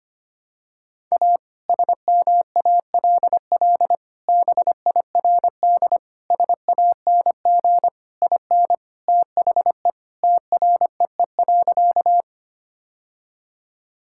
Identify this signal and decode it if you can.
Morse code, heard as a keyed tone, about 25 wpm, 700 Hz. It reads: A SMALL BIRD SANG IN THE TREE.
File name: practice / 25wpm / Sentences with 2-Word Phrases - 25wpm